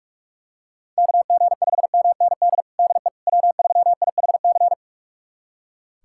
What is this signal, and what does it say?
Morse code, heard as a keyed tone, about 41 wpm, 700 Hz. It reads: KG5MND DE W3IHC
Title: USA Callsign DE Callsign 45wpm